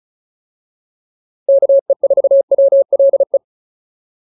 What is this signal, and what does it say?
Morse code, heard as a keyed tone, about 34 wpm, 560 Hz. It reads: KE4WLE